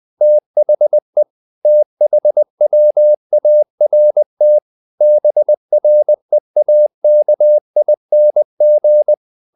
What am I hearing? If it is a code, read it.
Morse code, heard as a keyed tone, about 20 wpm, 600 Hz. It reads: THE THWART BREAKING
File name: the old man and the sea / the old man and the sea part 1 20wpm